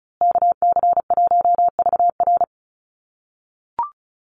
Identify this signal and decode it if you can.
Morse code, heard as a keyed tone, about 34 wpm, 700 Hz. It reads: KC1VR
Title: USA Call Signs Repeated 3x 35wpm